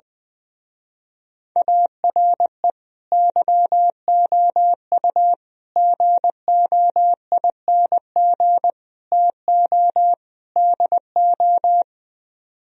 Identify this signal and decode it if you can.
Morse code, heard as a keyed tone, about 21 wpm, 700 Hz. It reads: ARE YOU GOING TO DO
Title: Speed-Racing - Sets of 5 Words 20wpm